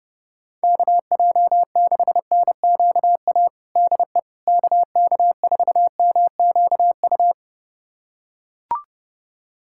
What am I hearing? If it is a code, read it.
Morse code, heard as a keyed tone, about 32 wpm, 700 Hz. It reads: KJ6NQA DE KK4MQU
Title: USA Callsign DE Callsign 30wpm